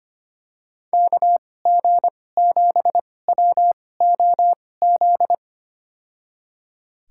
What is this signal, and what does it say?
Morse code, heard as a keyed tone, about 25 wpm, spaced as for 18 wpm, 700 Hz. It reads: KG7WOZ
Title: USA Call Signs - 2x Character Spacing - 25wpm